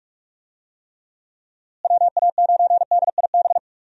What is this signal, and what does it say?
Morse code, heard as a keyed tone, about 43 wpm, 700 Hz. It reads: WA9DIB
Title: USA Call Signs - 6 Characters - Repeated 2x - 45wpm